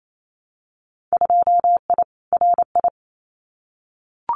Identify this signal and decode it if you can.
Morse code, heard as a keyed tone, about 28 wpm, 700 Hz. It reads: 2I RI
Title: Winter Field Day Exchanges Encore 28wpm